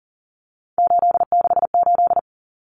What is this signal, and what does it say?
Morse code, heard as a keyed tone, about 40 wpm, 700 Hz. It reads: 868
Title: Numbers - 3-Digit 40wpm